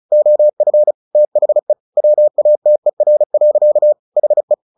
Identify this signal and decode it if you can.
Morse code, heard as a keyed tone, about 34 wpm, 600 Hz. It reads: OF THE WATER. HE
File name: the old man and the sea / the old man and the sea 1 35wpm